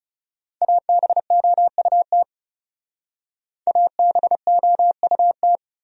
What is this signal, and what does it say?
Morse code, heard as a keyed tone, about 36 wpm, 700 Hz. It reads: ABOUT ABOUT
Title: Speed-Racing Top 200 Words 30wpm